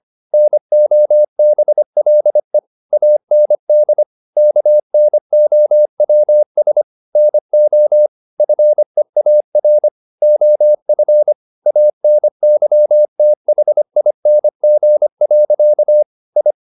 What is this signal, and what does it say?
Morse code, heard as a keyed tone, about 25 wpm, 600 Hz. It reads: NOBLE AND KNOWS NO FEAR OF ANYTHING. I